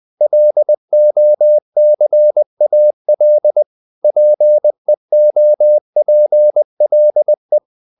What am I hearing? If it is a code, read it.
Morse code, heard as a keyed tone, about 20 wpm, 600 Hz. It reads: LOCAL PEOPLE
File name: the old man and the sea / the old man and the sea part 1 20wpm